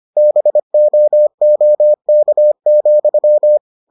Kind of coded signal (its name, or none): Morse code